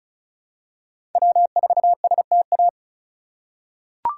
Morse code, heard as a keyed tone, about 34 wpm, 700 Hz. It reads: W4STA
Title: USA Call Signs - 3x Character Spacing - 35wpm